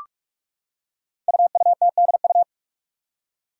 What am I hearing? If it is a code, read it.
Morse code, heard as a keyed tone, about 44 wpm, 700 Hz. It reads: UUTDU
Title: Code Groups - 5 Letters - 45wpm